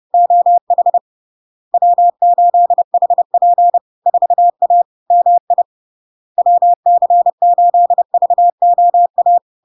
Morse code, heard as a keyed tone, about 30 wpm, 700 Hz. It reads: OH W8HP 4A MI WC8VOA
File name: LARK-FD-2022-Log-in-CW-30wpm-700-Hz